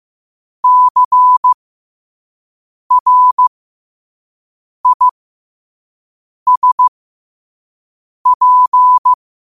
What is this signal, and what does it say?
Morse code, heard as a keyed tone, about 15 wpm, spaced as for 5 wpm, 1000 Hz. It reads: CRISP
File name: qotdfiles / 5wpm